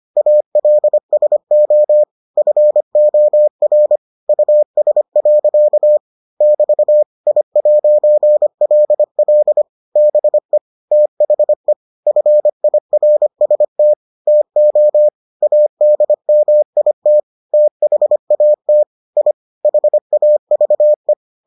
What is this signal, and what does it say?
Morse code, heard as a keyed tone, about 25 wpm, 600 Hz. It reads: ALSO FOR US. <BT> I'LL BE THE FIRST TO ADMIT THAT I HAVE